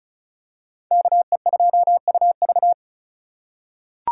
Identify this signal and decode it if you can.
Morse code, heard as a keyed tone, about 36 wpm, 700 Hz. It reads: KE2UV
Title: USA Call Signs - 5 Characters - 2x Character Spacing - 35wpm